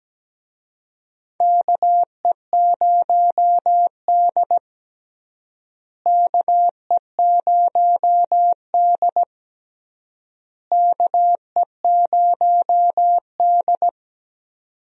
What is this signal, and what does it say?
Morse code, heard as a keyed tone, about 17 wpm, 700 Hz. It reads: KE0D KE0D KE0D
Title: USA Call Signs - 4 Characters - Repeated 3x - 17wpm